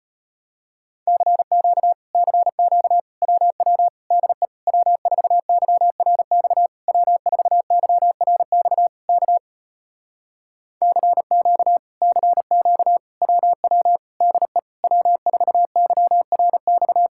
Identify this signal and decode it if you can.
Morse code, heard as a keyed tone, about 33 wpm, 700 Hz. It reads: CQ CQ WW DE W4YRX W4YRX K CQ CQ WW DE W4YRX